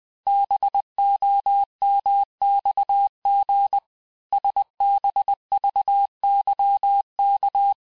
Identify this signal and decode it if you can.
Morse code, heard as a keyed tone, about 20 wpm, 800 Hz. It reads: BOMXG SBVYK
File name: cours-du-lundi files / cours-1584-teste-chiffre-vitesse-19mpm